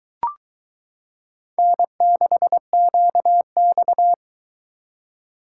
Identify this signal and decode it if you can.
Morse code, heard as a keyed tone, about 22 wpm, 700 Hz. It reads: N6QX